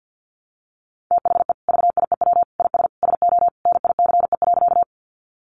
Morse code, heard as a keyed tone, about 50 wpm, 700 Hz. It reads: THE VIEW IS SO NICE.
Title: Additional Easy Phrases 50wpm